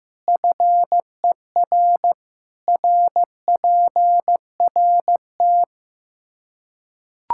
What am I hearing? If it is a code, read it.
Morse code, heard as a keyed tone, about 15 wpm, 700 Hz. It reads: FER RPRT
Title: Practice QSO - Exchange 2 - Info - 15wpm